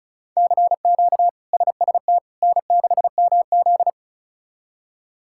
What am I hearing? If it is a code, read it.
Morse code, heard as a keyed tone, about 36 wpm, 700 Hz. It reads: CQ SST N6MZ
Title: K1USN SST QSO Practice - 35wpm